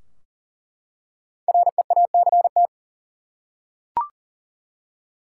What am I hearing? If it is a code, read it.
Morse code, heard as a keyed tone, about 40 wpm, 700 Hz. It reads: REACT
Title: Five Letter Words - 40wpm